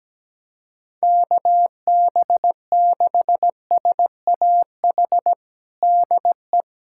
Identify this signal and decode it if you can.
Morse code, heard as a keyed tone, about 17 wpm, 700 Hz. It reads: KB6SAH DE
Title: USA Callsign DE Callsign 17wpm